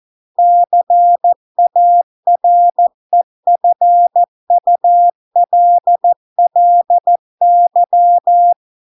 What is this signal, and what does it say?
Morse code, heard as a keyed tone, about 14 wpm, 700 Hz. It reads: CAREFULLY